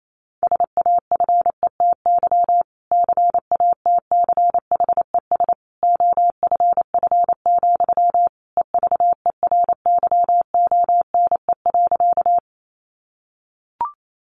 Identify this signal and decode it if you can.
Morse code, heard as a keyed tone, about 28 wpm, 700 Hz. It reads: SAFETY CATCHES OFF, EVERYONE.
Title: A Sound Of Thunder 28wpm